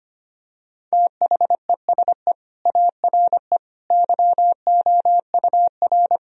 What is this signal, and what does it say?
Morse code, heard as a keyed tone, about 25 wpm, 700 Hz. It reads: THESE ARE YOUR
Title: Sentences from Top 1000 Words - Review Part 1 - 25wpm